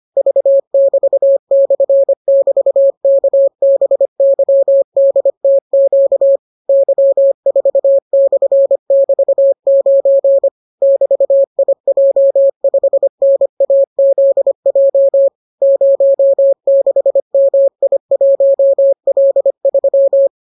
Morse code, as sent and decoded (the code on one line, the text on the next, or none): ...- -...- -..-. -...- -.- -... -.-- -.. - --.- / -.-- ....- -..-. -...- ----. / -...- .. .--- ..... -. .- --.. .--- / ----- -.... -- .. .---- .-.. ...--
V<BT>/<BT>KBYDTQ Y4/<BT>9 <BT>IJ5NAZJ 06MI1L3